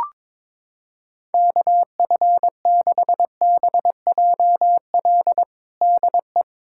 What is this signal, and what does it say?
Morse code, heard as a keyed tone, about 22 wpm, 700 Hz. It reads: KF6BJL DE
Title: USA Callsign DE Callsign - 6 Characters - 22wpm